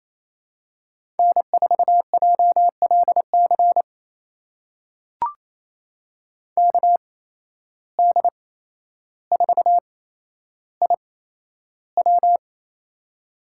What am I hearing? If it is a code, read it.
Morse code, heard as a keyed tone, about 28 wpm, spaced as for 12 wpm, 700 Hz. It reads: N4JLC KD4IW
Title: USA Call Signs - 5 Characters - 8x Character Spacing - 28wpm